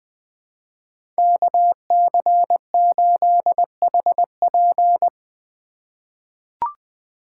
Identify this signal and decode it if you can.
Morse code, heard as a keyed tone, about 20 wpm, 700 Hz. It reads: KC8HP
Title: USA Call Signs - 5 Characters - 3x Character Spacing - 20wpm